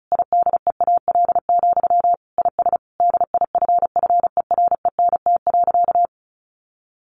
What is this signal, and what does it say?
Morse code, heard as a keyed tone, about 34 wpm, 700 Hz. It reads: IDEAL, IS DIFFERENT.
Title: Sentences from Top 1400 Words 35wpm